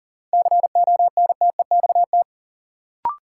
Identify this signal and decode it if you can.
Morse code, heard as a keyed tone, about 40 wpm, 700 Hz. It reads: CONTEXT
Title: Sets of 4 Words - New Word Drill - 40wpm